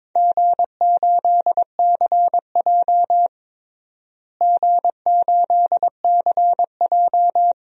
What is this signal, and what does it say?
Morse code, heard as a keyed tone, about 22 wpm, 700 Hz. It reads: G8CJ G8CJ
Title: DXCC-100 Call Signs - 4 Characters - Repeated 2x - 22wpm